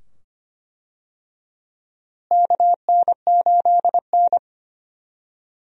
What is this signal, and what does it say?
Morse code, heard as a keyed tone, about 25 wpm, 700 Hz. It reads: KN8N
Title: USA Call Signs - 4 Characters - Repeated 2x - 25wpm